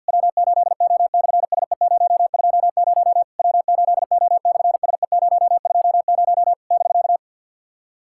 Morse code, heard as a keyed tone, about 50 wpm, 700 Hz. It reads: W8O/SE020 W8O/SE020 <BK>
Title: SOTA QSO Practice 50wpm